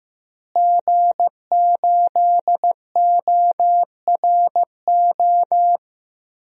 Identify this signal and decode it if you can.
Morse code, heard as a keyed tone, about 15 wpm, 700 Hz. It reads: G8ORO
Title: DXCC-100 Call Signs - 5 Characters - Repeated 3x - 15wpm